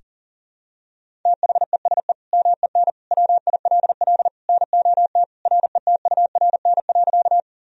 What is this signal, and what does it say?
Morse code, heard as a keyed tone, about 40 wpm, 700 Hz. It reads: THESE MEN WILL NOT RETURN.
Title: Sentences from Top 1000 Words - Review Part 1 - 40wpm